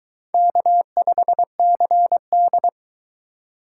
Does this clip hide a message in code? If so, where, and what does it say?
Yes: Morse code from 0.3 to 2.7 s, heard as a keyed tone, about 22 wpm, 700 Hz. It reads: K5CD